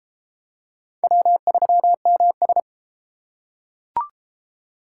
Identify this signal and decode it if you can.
Morse code, heard as a keyed tone, about 31 wpm, 700 Hz. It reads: W3MS